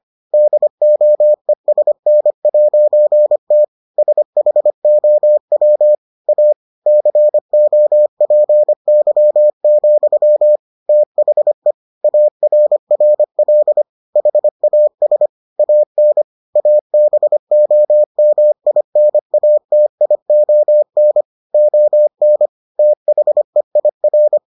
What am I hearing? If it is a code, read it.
Morse code, heard as a keyed tone, about 25 wpm, 600 Hz. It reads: DOESN'T SHOW A COPY, THE ARRL HAS AN ABOMINATION ON THEIR